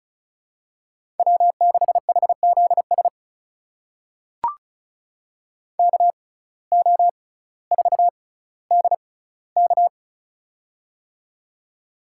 Morse code, heard as a keyed tone, about 34 wpm, spaced as for 18 wpm, 700 Hz. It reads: W6HZS KO4DK
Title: USA Call Signs - 5 Characters - 6x Character Spacing - 35wpm